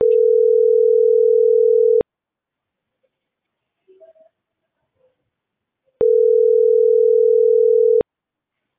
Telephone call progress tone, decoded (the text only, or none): ringback tone